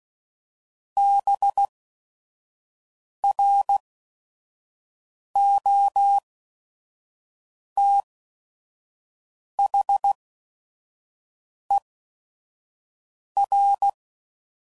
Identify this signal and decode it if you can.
Morse code, heard as a keyed tone, about 16 wpm, spaced as for 5 wpm, 800 Hz. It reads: BROTHER